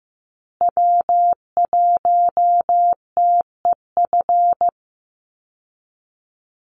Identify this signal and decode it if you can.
Morse code, heard as a keyed tone, about 15 wpm, 700 Hz. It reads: W1TEF
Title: Call Signs Repeated 2x - 15wpm